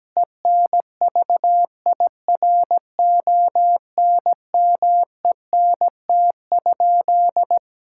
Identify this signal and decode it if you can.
Morse code, heard as a keyed tone, about 17 wpm, 700 Hz. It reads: ENVIRONMENT?
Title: Sentences from Top 1000 Words - Review Part 5 - 17wpm